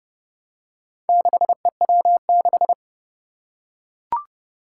Morse code, heard as a keyed tone, about 30 wpm, 690 Hz. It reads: 6EW6
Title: Four Challenging Letter-Number Combinations 30wpm